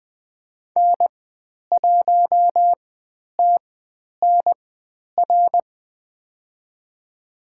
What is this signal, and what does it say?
Morse code, heard as a keyed tone, about 20 wpm, spaced as for 10 wpm, 700 Hz. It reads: N1TNR
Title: 095 Lesson 16g - Letter c - Call Signs 20wpm